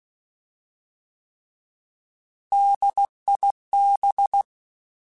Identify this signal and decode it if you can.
Morse code, heard as a keyed tone, about 16 wpm, 800 Hz. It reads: DIB
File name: Alphabetical / Alphabetical mp3s / Ds @15